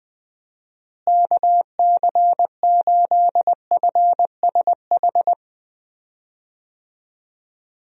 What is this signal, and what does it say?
Morse code, heard as a keyed tone, about 20 wpm, 700 Hz. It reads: KC8FSH
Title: USA Call Signs - 6 Characters - Repeated 2x - 20wpm